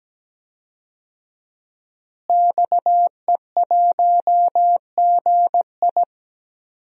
Morse code, heard as a keyed tone, about 17 wpm, 700 Hz. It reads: XE1GI
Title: DXCC-100 Call Signs - 5 Characters - Repeated 2x - 17wpm